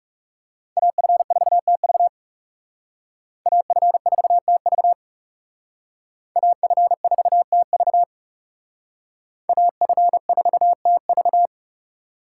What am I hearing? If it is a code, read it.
Morse code, heard as a keyed tone, about 37 wpm, 700 Hz. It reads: AF4TV AF4TV AF4TV AF4TV
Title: Speed Racing - USA 2x2 Call Signs 30wpm